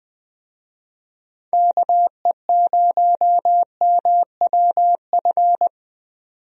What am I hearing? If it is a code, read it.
Morse code, heard as a keyed tone, about 20 wpm, 700 Hz. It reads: KE0MWF
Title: USA Call Signs - 6 Characters - 20wpm